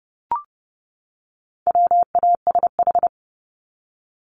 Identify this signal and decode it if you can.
Morse code, heard as a keyed tone, about 30 wpm, 700 Hz. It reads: WASH
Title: Top 1751-2000 Words 30wpm